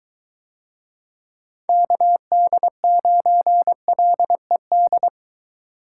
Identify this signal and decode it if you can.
Morse code, heard as a keyed tone, about 22 wpm, 700 Hz. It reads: KD9LED